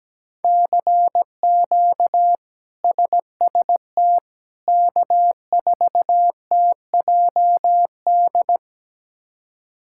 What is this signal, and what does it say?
Morse code, heard as a keyed tone, about 17 wpm, 700 Hz. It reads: CQ SST K4TJD
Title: K1USN SST QSO Practice - 17wpm